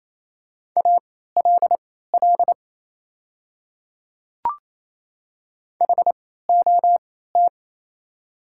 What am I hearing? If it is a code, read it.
Morse code, heard as a keyed tone, about 27 wpm, spaced as for 16 wpm, 700 Hz. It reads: ALL HOT
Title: 3x Character Spacing - Top 200 Words - 28wpm